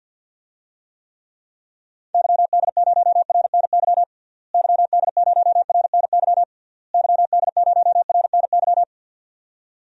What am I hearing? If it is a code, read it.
Morse code, heard as a keyed tone, about 49 wpm, 700 Hz. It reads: YD0RNY YD0RNY YD0RNY
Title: DXCC-100 Call Signs - 6 Characters - Repeated 3x - 50wpm